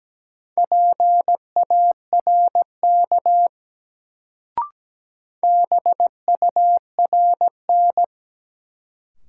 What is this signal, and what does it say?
Morse code, heard as a keyed tone, about 17 wpm, 700 Hz. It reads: PARK BURN
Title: Sentences from Top 1300 Words - New Word Drill Alt - 17wpm